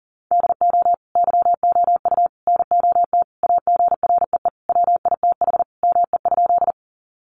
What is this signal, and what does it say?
Morse code, heard as a keyed tone, about 40 wpm, 700 Hz. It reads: DO YOU NOT AGREE WITH ME?
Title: Sentences from Top 1100 Words 40wpm